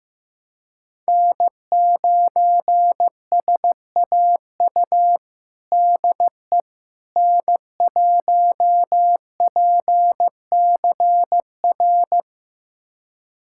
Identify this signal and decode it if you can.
Morse code, heard as a keyed tone, about 15 wpm, 700 Hz. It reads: N9SAU DE N1PCR
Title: USA Callsign DE Callsign - 5 Characters - 15wpm